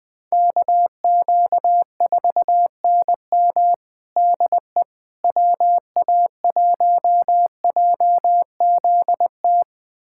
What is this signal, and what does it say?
Morse code, heard as a keyed tone, about 21 wpm, 700 Hz. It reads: KQ4NM DE WA1JZT
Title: USA Callsign DE Callsign 20wpm